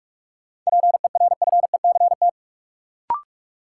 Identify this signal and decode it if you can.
Morse code, heard as a keyed tone, about 46 wpm, 700 Hz. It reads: PERFECT
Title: Sentences from Top 1000 Words - New Word Drill - 45wpm